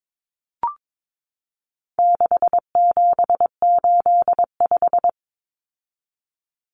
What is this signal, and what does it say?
Morse code, heard as a keyed tone, about 22 wpm, 700 Hz. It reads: 6785
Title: Common Cut Numbers - 4-Digit 22wpm